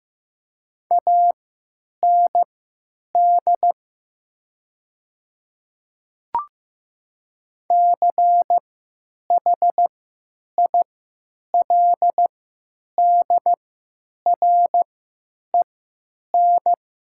Morse code, heard as a keyed tone, about 15 wpm, spaced as for 9 wpm, 700 Hz. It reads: AND CHILDREN
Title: 3x Character Spacing - Top 300 Words - 15wpm